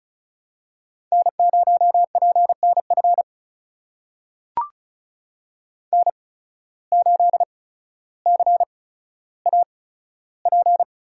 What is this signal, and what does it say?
Morse code, heard as a keyed tone, about 36 wpm, spaced as for 15 wpm, 700 Hz. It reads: N0PNF N8CAP
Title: USA Call Signs - 5 Characters - 8x Character Spacing - 35wpm